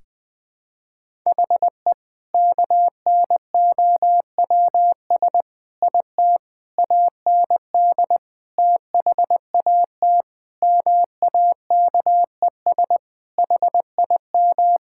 Morse code, heard as a keyed tone, about 20 wpm, 700 Hz. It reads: HE KNOWS IT AND THAT MAKES HIM
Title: Sentences with 2-Word Phrases - 20wpm